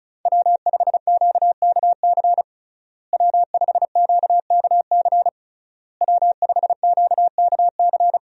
Morse code, heard as a keyed tone, about 36 wpm, 700 Hz. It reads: W5QKC W5QKC W5QKC
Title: USA Call Signs - 5 Characters - Repeated 3x - 35wpm